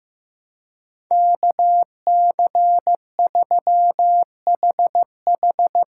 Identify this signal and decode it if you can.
Morse code, heard as a keyed tone, about 15 wpm, 700 Hz. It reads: KC3HH
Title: USA Call Signs - 6 Characters - 6x Character Spacing - 15wpm